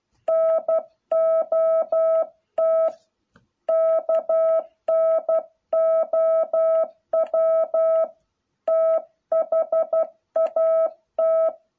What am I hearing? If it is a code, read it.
Morse code, heard as a keyed tone, about 12 wpm, 660 Hz. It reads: NOT KNOW THAT